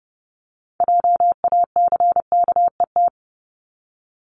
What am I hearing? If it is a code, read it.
Morse code, heard as a keyed tone, about 30 wpm, 700 Hz. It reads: JACKET
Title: Multi-letter Phonemes 30wpm